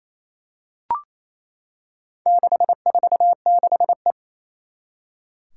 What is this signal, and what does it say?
Morse code, heard as a keyed tone, about 27 wpm, 700 Hz. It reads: 646E
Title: Four Challenging Letter-Number Combinations 28wpm